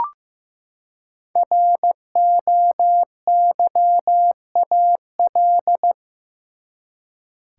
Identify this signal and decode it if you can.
Morse code, heard as a keyed tone, about 16 wpm, 700 Hz. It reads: ROYAL
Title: Five Letter Words - 15wpm